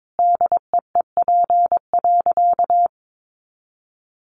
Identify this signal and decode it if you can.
Morse code, heard as a keyed tone, about 22 wpm, 700 Hz. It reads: DEEP.
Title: Sentences from Top 500 Words 22wpm